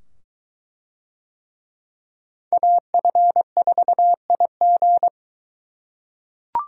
Morse code, heard as a keyed tone, about 22 wpm, 700 Hz. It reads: AF4IG